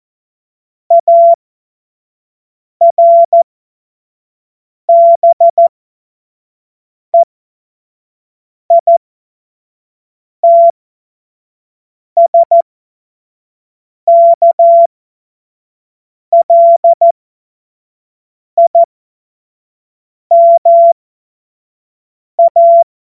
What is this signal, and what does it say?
Morse code, heard as a keyed tone, about 14 wpm, spaced as for 5 wpm, 680 Hz. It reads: ARBEITSKLIMA